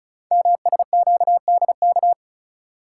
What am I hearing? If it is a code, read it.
Morse code, heard as a keyed tone, about 36 wpm, 690 Hz. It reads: MSQDK
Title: Code Groups - 5 Letters - 35wpm